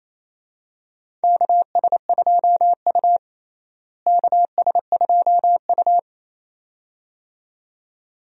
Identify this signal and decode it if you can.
Morse code, heard as a keyed tone, about 28 wpm, 700 Hz. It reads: KS2U KS2U
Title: USA Call Signs - 4 Characters - Repeated 2x - 28wpm